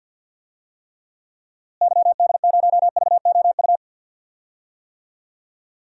Morse code, heard as a keyed tone, about 51 wpm, 700 Hz. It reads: YD0VOU
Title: DXCC-100 Call Signs - 6 Characters - Repeated 3x - 50wpm